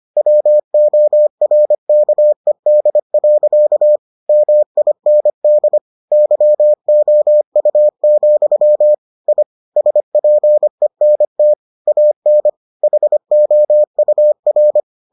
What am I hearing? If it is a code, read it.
Morse code, heard as a keyed tone, about 25 wpm, 600 Hz. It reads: WORKED. MIND YOU, I SPENT AN HOUR